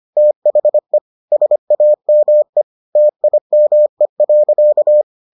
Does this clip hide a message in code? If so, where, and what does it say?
Yes: Morse code from 0.2 to 5.0 s, heard as a keyed tone, about 25 wpm, 600 Hz. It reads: THE SAME TIME.